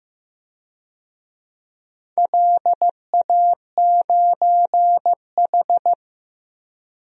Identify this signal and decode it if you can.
Morse code, heard as a keyed tone, about 15 wpm, 700 Hz. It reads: LA9H
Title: DXCC-100 Call Signs - 4 Characters - 15wpm